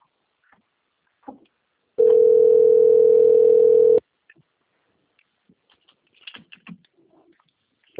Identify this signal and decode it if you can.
Ringback tone (North America), 440 + 480 Hz.